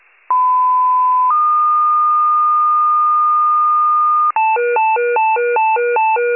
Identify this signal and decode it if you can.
Two-tone page, heard as a one-second tone, then a three-second tone, about 1010 then 1220 Hz. It is heard from 0.3 to 4.3 s.